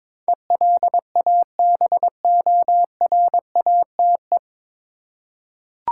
Morse code, heard as a keyed tone, about 22 wpm, 700 Hz. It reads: ELABORATE
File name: 9 Letter Words - First 3 Letters Given - 22wpm